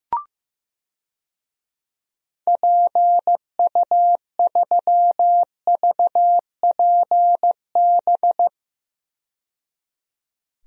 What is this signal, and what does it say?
Morse code, heard as a keyed tone, about 15 wpm, 700 Hz. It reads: PU3VPB
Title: DXCC-100 Call Signs - 6 Characters - 15wpm